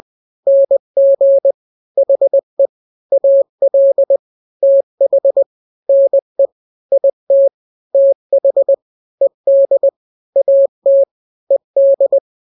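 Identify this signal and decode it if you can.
Morse code, heard as a keyed tone, about 20 wpm, 550 Hz. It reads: NG HE AL TH NE IT TH ED AT ED
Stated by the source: Morse drill of two-letter n-grams at 20/18 WPM (Farnsworth), from KK7RR.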